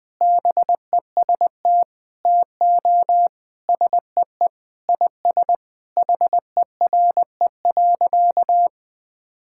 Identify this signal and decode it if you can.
Morse code, heard as a keyed tone, about 20 wpm, 700 Hz. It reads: BEST TO SEE IS HERE.
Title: Sentences with 4-Word Phrases - 20wpm